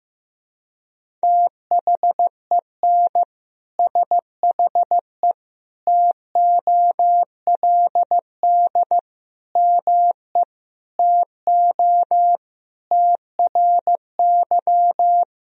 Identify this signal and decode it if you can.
Morse code, heard as a keyed tone, about 16 wpm, 700 Hz. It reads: THEN SHE TOLD ME TO TRY
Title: Sentences from Top 400 Words and Abbreviations 15wpm